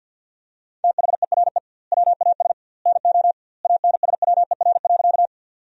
Morse code, heard as a keyed tone, about 50 wpm, 700 Hz. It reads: THERE WAS NO ANSWER.
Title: 2x Word-Spacing - Sentences from Top 300 Words - 50wpm